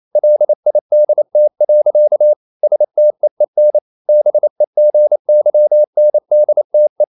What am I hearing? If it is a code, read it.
Morse code, heard as a keyed tone, about 28 wpm, 600 Hz. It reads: LIDT. STEEN BEGYNDTE
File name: QRQ 28wpm